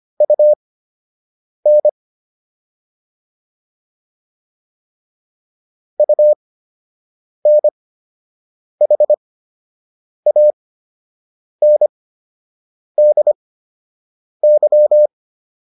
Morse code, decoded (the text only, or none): UN UNHANDY